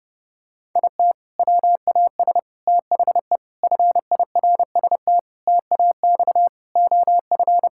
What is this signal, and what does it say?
Morse code, heard as a keyed tone, about 30 wpm, 700 Hz. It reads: IT WAS THE FIRST TAX OF
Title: Sentences from Top 1000 Words - Review Part 5 - 30wpm